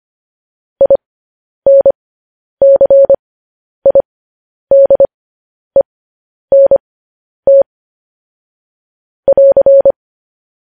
Morse code, heard as a keyed tone, about 25 wpm, spaced as for 10 wpm, 550 Hz. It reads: INCIDENT <AR>
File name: podcast / News headlines-1x-25x10